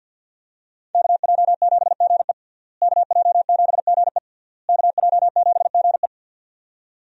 Morse code, heard as a keyed tone, about 46 wpm, 700 Hz. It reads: KJ7GE KJ7GE KJ7GE
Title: USA Call Signs - 5 Characters - Repeated 3x - 50wpm